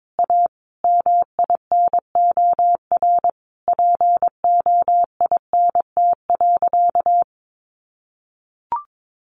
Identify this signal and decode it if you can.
Morse code, heard as a keyed tone, about 22 wpm, 700 Hz. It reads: A MINOR POINT.
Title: Sentences from Top 1500 Words 22wpm